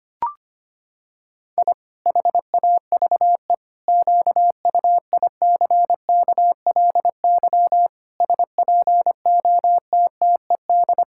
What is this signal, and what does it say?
Morse code, heard as a keyed tone, about 25 wpm, 700 Hz. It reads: I HAVE QUICKLY SPOTTED
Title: Pangram Challenge v2 - 25wpm